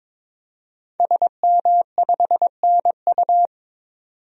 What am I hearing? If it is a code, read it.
Morse code, heard as a keyed tone, about 22 wpm, 700 Hz. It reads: SM5NU
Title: DXCC-100 Call Signs - 5 Characters - 22wpm